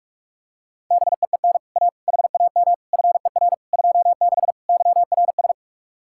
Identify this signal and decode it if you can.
Morse code, heard as a keyed tone, about 36 wpm, 700 Hz. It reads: BEEN A HAM FER 26 YRS
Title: Speed-Racing Common QSO Phrases - 30wpm